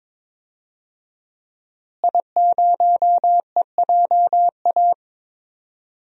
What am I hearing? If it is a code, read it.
Morse code, heard as a keyed tone, about 22 wpm, 700 Hz. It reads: I0EJA
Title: DXCC-100 Call Signs - 5 Characters - Repeated 3x - 22wpm